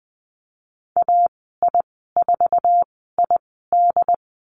Morse code, heard as a keyed tone, about 20 wpm, spaced as for 15 wpm, 700 Hz. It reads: AI4ID
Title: 088 Lesson 15g - Number 5 - Call Signs 20wpm